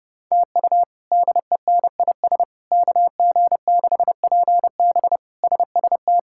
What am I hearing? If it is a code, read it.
Morse code, heard as a keyed tone, about 32 wpm, 700 Hz. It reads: TU DENIS KG6PB SST
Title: K1USN SST QSO Practice - 30wpm